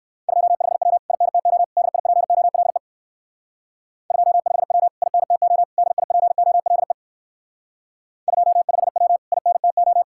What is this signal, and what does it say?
Morse code, heard as a keyed tone, about 58 wpm, 700 Hz. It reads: 25W INTO DIPOLE 25W INTO DIPOLE 25W INTO